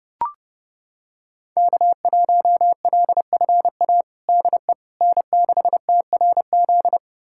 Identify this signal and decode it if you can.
Morse code, heard as a keyed tone, about 30 wpm, 700 Hz. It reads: K1LFA DE N6TRZ
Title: USA Callsign DE Callsign - 5 Characters - 30wpm